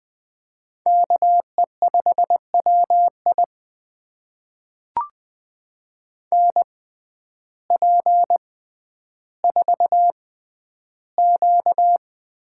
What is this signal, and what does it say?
Morse code, heard as a keyed tone, about 21 wpm, spaced as for 11 wpm, 700 Hz. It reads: KE5WI NP4Q
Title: USA Call Signs - 5 Characters - 6x Character Spacing - 20wpm